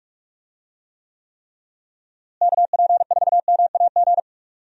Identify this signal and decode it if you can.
Morse code, heard as a keyed tone, about 41 wpm, 700 Hz. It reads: KP4MAG
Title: USA Call Signs - 6 Characters - Repeated 3x - 45wpm